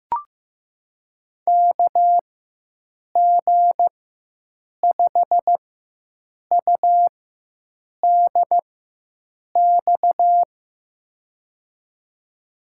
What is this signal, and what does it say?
Morse code, heard as a keyed tone, about 15 wpm, spaced as for 7 wpm, 700 Hz. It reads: KG5UDX
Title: USA Call Signs - 6 Characters - 4x Character Spacing - 15wpm